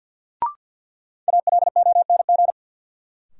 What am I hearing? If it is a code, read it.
Morse code, heard as a keyed tone, about 49 wpm, 700 Hz. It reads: ALONG
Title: Sentences from Top 700 Words - New Word Drill - 50wpm